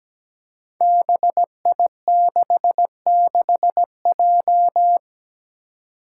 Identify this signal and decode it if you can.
Morse code, heard as a keyed tone, about 17 wpm, 700 Hz. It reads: BI66J
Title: Five Challenging Letter-Number Combinations 17wpm